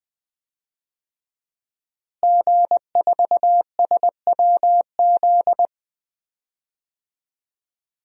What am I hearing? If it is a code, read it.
Morse code, heard as a keyed tone, about 20 wpm, 700 Hz. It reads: G4SWZ
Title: DXCC-100 Call Signs - 5 Characters - Repeated 3x - 20wpm